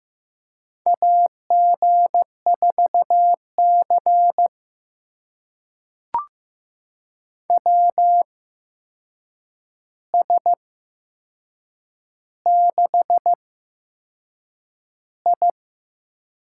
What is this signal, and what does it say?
Morse code, heard as a keyed tone, about 15 wpm, spaced as for 6 wpm, 700 Hz. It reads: AG4C WS6I
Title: USA Call Signs - 4 Characters - 8x Character Spacing - 15wpm